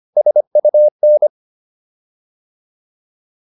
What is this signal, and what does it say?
Morse code, heard as a keyed tone, about 25 wpm, 600 Hz. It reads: SUN
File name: PR405 25